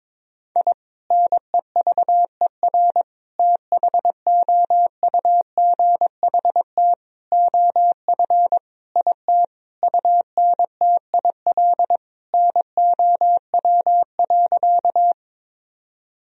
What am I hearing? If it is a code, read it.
Morse code, heard as a keyed tone, about 21 wpm, 700 Hz. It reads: I NEVER THOUGHT OF IT UNTIL NOW.
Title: Sentences from Top 1000 Words - Review Part 1 - 22wpm